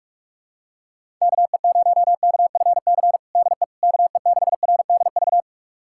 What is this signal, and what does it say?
Morse code, heard as a keyed tone, about 41 wpm, 700 Hz. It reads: KE0KFC DE KE6RDV
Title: USA Callsign DE Callsign - 6 Characters - 45wpm